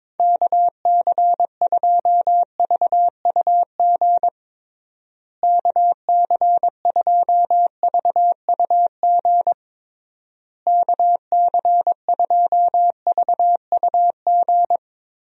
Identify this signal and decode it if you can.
Morse code, heard as a keyed tone, about 22 wpm, 700 Hz. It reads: KC2VUG KC2VUG KC2VUG
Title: USA Call Signs - 6 Characters - Repeated 3x - 22wpm